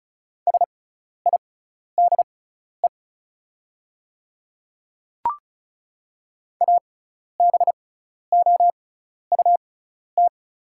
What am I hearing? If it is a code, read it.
Morse code, heard as a keyed tone, about 36 wpm, spaced as for 12 wpm, 700 Hz. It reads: SIDE ABOUT
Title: 6x Character Spacing - Top 100 Words - 35wpm